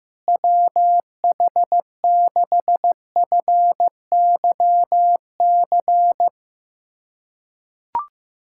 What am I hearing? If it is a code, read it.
Morse code, heard as a keyed tone, about 15 wpm, 700 Hz. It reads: WH6FYC
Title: USA Callsign DE Callsign 15wpm